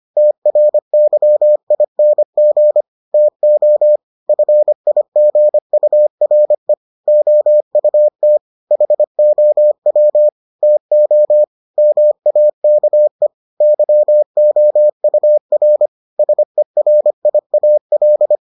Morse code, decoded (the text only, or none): TRYING TO FIGURE OUT HOW TO MAKE YOUR SERIAL